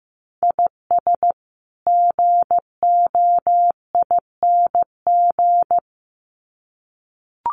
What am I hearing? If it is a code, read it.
Morse code, heard as a keyed tone, about 15 wpm, 700 Hz. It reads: IS GOING
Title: Sets of 2 Words 15wpm